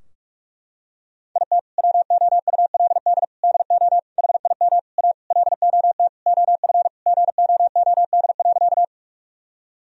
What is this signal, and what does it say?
Morse code, heard as a keyed tone, about 43 wpm, 700 Hz. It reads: IT WOULD DO HIM A LOT OF GOOD.
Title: Sentences from Top 1000 Words - Review Part 3 - 45wpm